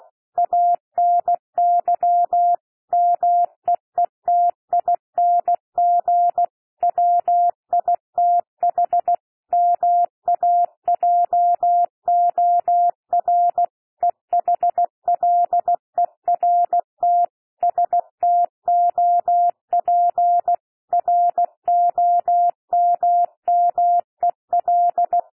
Morse code, heard as a keyed tone, about 16 wpm, 700 Hz. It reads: ANY MEETING WITH MAJOR EHLERT STOP ROMMEL
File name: cwmp3 / P1 SOUND PASSAGE 16 wpm